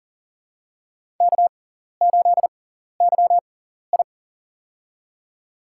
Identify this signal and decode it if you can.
Morse code, heard as a keyed tone, about 40 wpm, spaced as for 14 wpm, 700 Hz. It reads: K8YI
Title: USA Call Signs - 6x Character Spacing - 40wpm